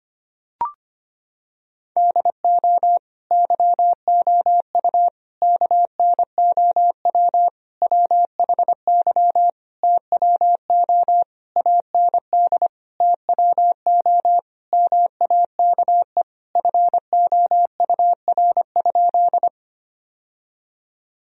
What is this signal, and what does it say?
Morse code, heard as a keyed tone, about 25 wpm, 700 Hz. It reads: DO YOU KNOW WHY TWO AND TWO MAKE FOUR?
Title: Sentences from Top 1000 Words - Review Part 1 - 25wpm